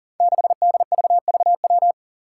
Morse code, heard as a keyed tone, about 40 wpm, 700 Hz. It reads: 6DVVW
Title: Five Challenging Letter-Number Combinations 40wpm